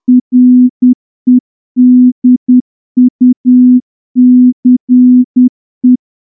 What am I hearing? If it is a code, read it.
Morse code, heard as a keyed tone, about 10 wpm, 250 Hz. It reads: REDUCE
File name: morse-training-10WPM